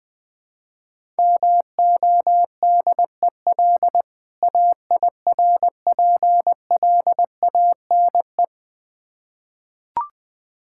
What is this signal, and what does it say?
Morse code, heard as a keyed tone, about 20 wpm, 690 Hz. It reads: MODEL AIRPLANE
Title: Difficult Words with Hints - 20wpm